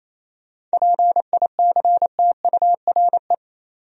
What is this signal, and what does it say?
Morse code, heard as a keyed tone, about 33 wpm, 700 Hz. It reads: PICTURE
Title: Speed-Racing Top 300 Words 28wpm